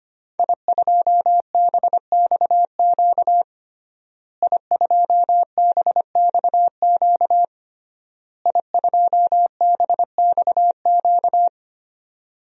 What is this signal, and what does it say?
Morse code, heard as a keyed tone, about 25 wpm, 700 Hz. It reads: I2BXQ I2BXQ I2BXQ
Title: DXCC-100 Call Signs - 5 Characters - Repeated 3x - 25wpm